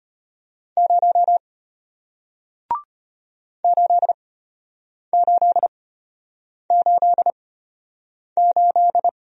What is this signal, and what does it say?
Morse code, heard as a keyed tone, about 34 wpm, spaced as for 8 wpm, 700 Hz. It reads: 0 8888